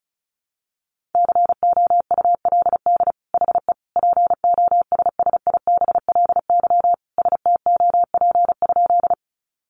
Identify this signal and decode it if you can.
Morse code, heard as a keyed tone, about 34 wpm, 700 Hz. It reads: COULD HE POSSIBLY STOP?
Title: Sentences from Top 1300 Words 35wpm